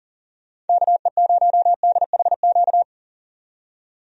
Morse code, heard as a keyed tone, about 37 wpm, 700 Hz. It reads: KE0DHQ
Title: USA Call Signs - 6 Characters - 40wpm